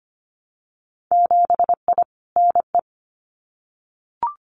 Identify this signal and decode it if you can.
Morse code, heard as a keyed tone, about 25 wpm, 700 Hz. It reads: 7I NE
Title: Winter Field Day Exchanges Encore 25wpm